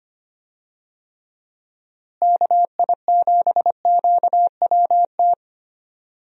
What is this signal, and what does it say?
Morse code, heard as a keyed tone, about 25 wpm, 700 Hz. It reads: KI7QWT
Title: USA Call Signs - 6 Characters - Repeated 2x - 25wpm